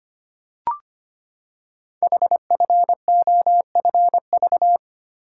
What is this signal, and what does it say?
Morse code, heard as a keyed tone, about 25 wpm, 700 Hz. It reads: HFOFV
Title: Code Groups - 5 Letters - 25wpm